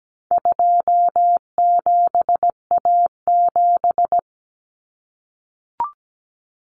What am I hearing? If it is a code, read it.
Morse code, heard as a keyed tone, about 17 wpm, 700 Hz. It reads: 27A7
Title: Common Cut Numbers - 4-Digit 17wpm